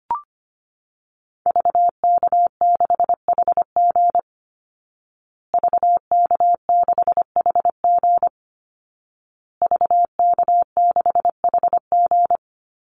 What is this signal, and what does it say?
Morse code, heard as a keyed tone, about 25 wpm, 700 Hz. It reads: VK6HG VK6HG VK6HG
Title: Call Signs Repeated 2x - 25wpm